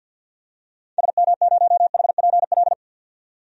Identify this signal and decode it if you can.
Morse code, heard as a keyed tone, about 52 wpm, 700 Hz. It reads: SM0HPL
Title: DXCC-100 Call Signs - 6 Characters - 50wpm